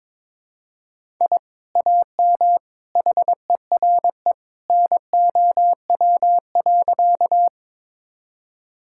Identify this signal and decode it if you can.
Morse code, heard as a keyed tone, about 22 wpm, 700 Hz. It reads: I AM HERE NOW.
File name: practice / 22wpm / Sentences with 2-Word Phrases - 22wpm